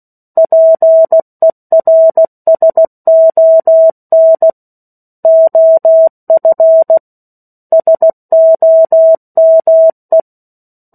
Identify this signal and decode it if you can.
Morse code, heard as a keyed tone, about 16 wpm, 650 Hz. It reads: PERSON OF SOME
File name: 16 WPM